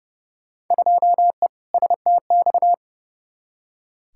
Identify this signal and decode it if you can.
Morse code, heard as a keyed tone, about 29 wpm, 700 Hz. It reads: 2E STX
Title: Field Day Exchanges Encore - 30wpm